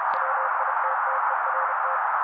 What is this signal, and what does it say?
Morse code, heard as a keyed tone, about 32 wpm, 530 Hz. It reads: WITTERT
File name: daily / 31wpm noise